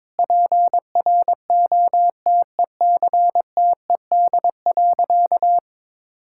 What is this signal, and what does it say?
Morse code, heard as a keyed tone, about 21 wpm, 700 Hz. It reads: PROTECTED.
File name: Sentences with 9-Word Phrases - 22wpm